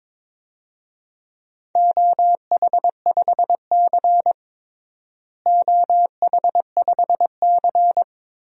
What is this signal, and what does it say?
Morse code, heard as a keyed tone, about 22 wpm, 700 Hz. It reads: OH5C OH5C
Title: DXCC-100 Call Signs - 4 Characters - Repeated 3x - 22wpm